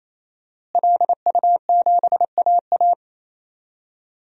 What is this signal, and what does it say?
Morse code, heard as a keyed tone, about 27 wpm, 700 Hz. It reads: LU7AA
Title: DXCC-100 Call Signs - 5 Characters - Repeated 3x - 28wpm